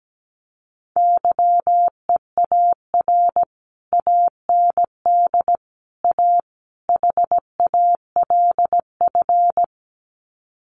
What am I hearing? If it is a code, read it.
Morse code, heard as a keyed tone, about 17 wpm, 700 Hz. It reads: YEAR AND A HALF
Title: Sets of 4 Words 17wpm